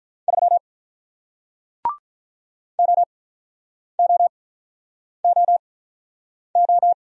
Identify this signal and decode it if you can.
Morse code, heard as a keyed tone, about 47 wpm, spaced as for 9 wpm, 700 Hz. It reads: 3 OOOO